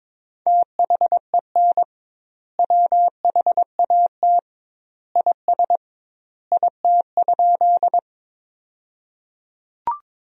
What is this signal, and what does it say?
Morse code, heard as a keyed tone, about 22 wpm, 700 Hz. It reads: THEN WHAT IS IT?
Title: 2x Word-Spacing - Sentences from Top 100 Words - 22wpm